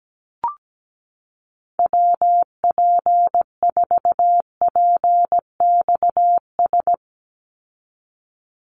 Morse code, heard as a keyed tone, about 17 wpm, 700 Hz. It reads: WP4PXS
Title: USA Call Signs 17wpm